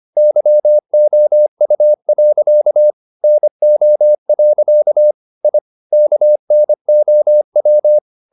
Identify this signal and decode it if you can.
Morse code, heard as a keyed tone, about 25 wpm, 600 Hz. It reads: YOU. NO. I KNOW
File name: the old man and the sea part 1 25wpm